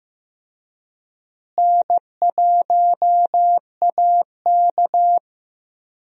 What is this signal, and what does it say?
Morse code, heard as a keyed tone, about 16 wpm, 700 Hz. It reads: N1AK
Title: USA Call Signs - 4 Characters - Repeated 3x - 15wpm